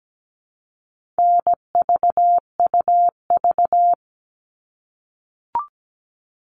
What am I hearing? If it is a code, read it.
Morse code, heard as a keyed tone, about 17 wpm, 700 Hz. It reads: NVUV
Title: All Cut Numbers - 4-Digit 17wpm